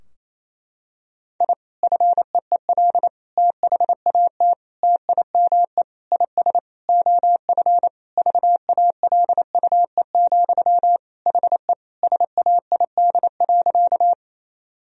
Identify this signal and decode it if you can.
Morse code, heard as a keyed tone, about 27 wpm, 700 Hz. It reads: I FEEL THAT TIME IS OF VALUE, HE SAID.
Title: Sentences from Top 1000 Words - Review Part 3 - 28wpm